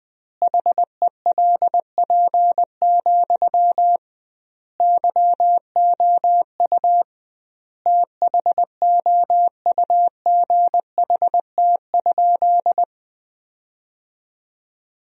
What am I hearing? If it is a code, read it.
Morse code, heard as a keyed tone, about 21 wpm, 700 Hz. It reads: HELP, YOU THOUGHT?
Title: 2x Word-Spacing - Sentences from Top 300 Words - 20wpm